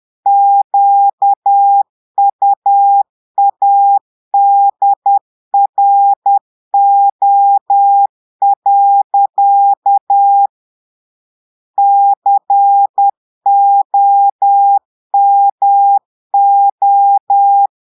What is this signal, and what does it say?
Morse code, heard as a keyed tone, about 10 wpm, 800 Hz. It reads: QUADRO. COMO
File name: Portugais - texte en clair 1 - vitesse - 10 mots minute